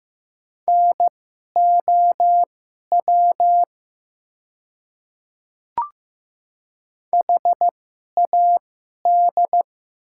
Morse code, heard as a keyed tone, about 15 wpm, spaced as for 11 wpm, 700 Hz. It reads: NOW HAD
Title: 2x Character Spacing - Top 500 Words - 15wpm